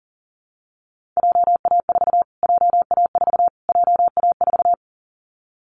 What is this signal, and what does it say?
Morse code, heard as a keyed tone, about 40 wpm, 700 Hz. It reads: JA4 JA4 JA4
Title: Callsign Prefixes for Top 25 DXCC - Repeated 3x - 40wpm